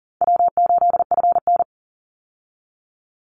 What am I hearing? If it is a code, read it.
Morse code, heard as a keyed tone, about 40 wpm, 700 Hz. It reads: W8FN
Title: Easy Call Signs Repeated 3x - 40wpm